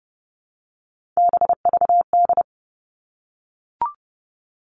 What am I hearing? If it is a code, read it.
Morse code, heard as a keyed tone, about 30 wpm, 700 Hz. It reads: BVD
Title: Three Challenging Letter-Number Combinations 30wpm